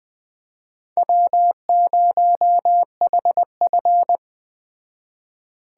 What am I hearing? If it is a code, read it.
Morse code, heard as a keyed tone, about 20 wpm, 700 Hz. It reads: W0HF
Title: Speed Racing - USA 1x2 Call Signs 17wpm